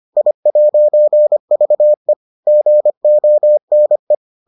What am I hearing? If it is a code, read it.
Morse code, heard as a keyed tone, about 25 wpm, 600 Hz. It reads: I'VE GONE